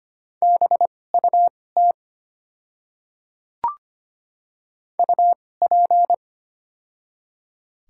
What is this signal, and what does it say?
Morse code, heard as a keyed tone, about 25 wpm, spaced as for 18 wpm, 690 Hz. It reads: BUT UP
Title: 2x Character Spacing - Top 100 Words - 25wpm